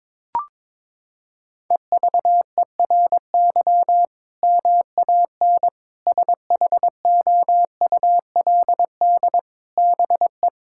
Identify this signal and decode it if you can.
Morse code, heard as a keyed tone, about 22 wpm, 700 Hz. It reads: EVERY MAN SHOULD BE
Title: Sentences from Top 1000 Words - Review Part 6 - 22wpm